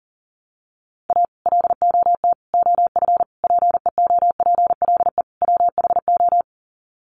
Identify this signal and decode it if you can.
Morse code, heard as a keyed tone, about 40 wpm, 700 Hz. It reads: A LOT OF PEOPLE WHO
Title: Sets of 5 Words 40wpm